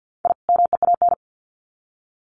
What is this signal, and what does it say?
Morse code, heard as a keyed tone, about 50 wpm, 700 Hz. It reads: I MEAN
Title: Sets of 2 Words 50wpm